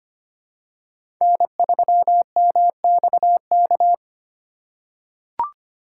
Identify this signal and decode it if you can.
Morse code, heard as a keyed tone, about 25 wpm, 700 Hz. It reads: N3MXK
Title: USA Call Signs - 5 Characters - 2x Character Spacing - 25wpm